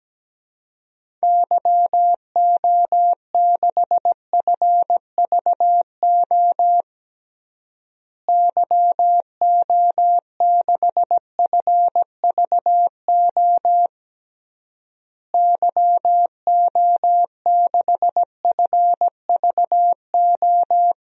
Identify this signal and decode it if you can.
Morse code, heard as a keyed tone, about 17 wpm, 700 Hz. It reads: YO6FVO YO6FVO YO6FVO
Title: DXCC-100 Call Signs - 6 Characters - Repeated 3x - 17wpm